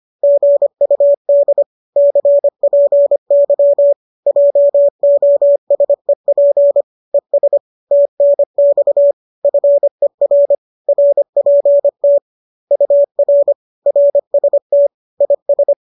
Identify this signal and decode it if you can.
Morse code, heard as a keyed tone, about 25 wpm, 570 Hz. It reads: GUD CPY JOSEP ES TNX FER RPT UR RST IS